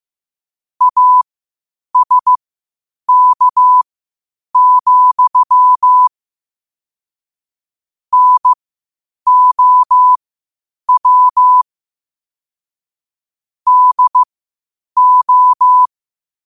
Morse code, heard as a keyed tone, about 15 wpm, spaced as for 8 wpm, 1000 Hz. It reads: ASK, NOW DO
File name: qotdfiles / 8wpm